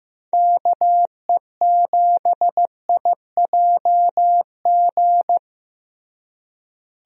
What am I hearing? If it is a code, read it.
Morse code, heard as a keyed tone, about 15 wpm, 700 Hz. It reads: KE7IJG
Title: USA Call Signs - 6 Characters - Repeated 2x - 15wpm